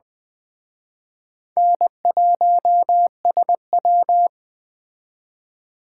Morse code, heard as a keyed tone, about 20 wpm, 700 Hz. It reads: N1SW